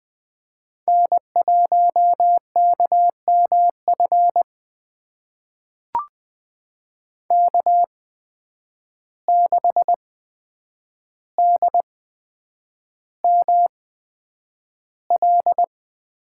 Morse code, heard as a keyed tone, about 20 wpm, spaced as for 9 wpm, 700 Hz. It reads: N1KMF K6DML
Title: USA Call Signs - 8x Character Spacing - 20wpm